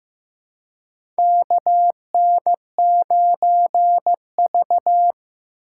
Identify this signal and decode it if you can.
Morse code, heard as a keyed tone, about 15 wpm, 700 Hz. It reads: KN9V
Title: USA Call Signs - 4 Characters - 3x Character Spacing - 15wpm